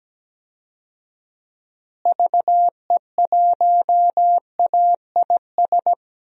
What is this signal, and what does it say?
Morse code, heard as a keyed tone, about 17 wpm, 700 Hz. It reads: VE1AIS
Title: DXCC-100 Call Signs - 6 Characters - Repeated 2x - 17wpm